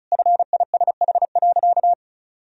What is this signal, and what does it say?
Morse code, heard as a keyed tone, about 34 wpm, 700 Hz. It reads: FISH.
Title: Sentences from Top 300 Words and Abbreviations 35wpm